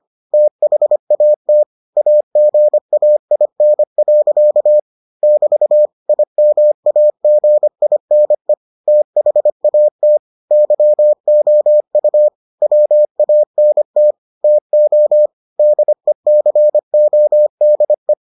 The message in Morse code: THAT AGAIN. <BT> IMAGINE THAT YOU WANT TO DECODE

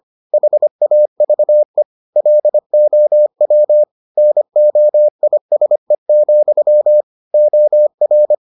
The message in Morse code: HAVE LOW NOISE, OR

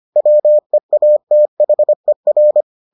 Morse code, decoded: WEATHER

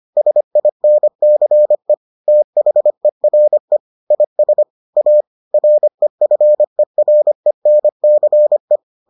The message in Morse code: SINCE THERE IS A REFERENCE

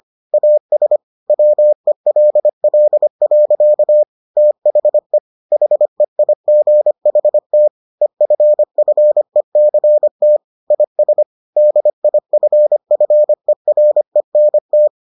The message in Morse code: AS WELL. THE HEIGHT EFFECT IS DIFFERENT

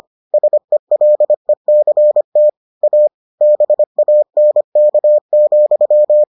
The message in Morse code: SELECT A BANK,

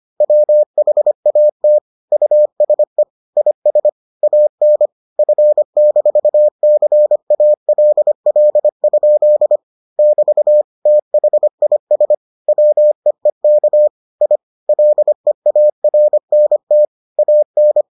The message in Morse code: WHAT USE IS AN F-CALL? <BT> THIS WEEK I LEARNT AN

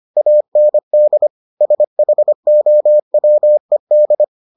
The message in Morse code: AND SHOWED

